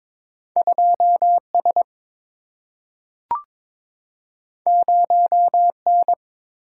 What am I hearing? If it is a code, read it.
Morse code, heard as a keyed tone, about 22 wpm, 700 Hz. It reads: 2 S 0 N
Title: Single-Letter Single-Number 22wpm